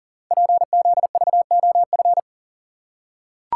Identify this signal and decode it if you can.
Morse code, heard as a keyed tone, about 39 wpm, 700 Hz. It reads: PZVOF
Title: Code Groups - 5 Letters - 40wpm